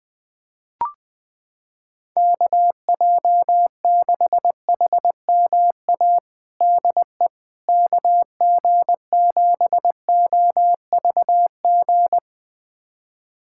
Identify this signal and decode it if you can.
Morse code, heard as a keyed tone, about 20 wpm, 700 Hz. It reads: KJ6HMA DE KG7OVG
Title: USA Callsign DE Callsign - 6 Characters - 20wpm